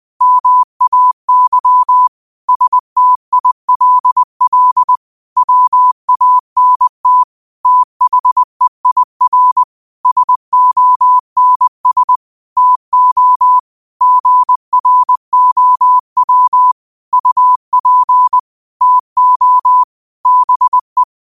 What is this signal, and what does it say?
Morse code, heard as a keyed tone, about 20 wpm, 1000 Hz. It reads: MAY STILL WANT THEIR SONS TO GROW UP TO BE
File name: qotdfiles / 20wpm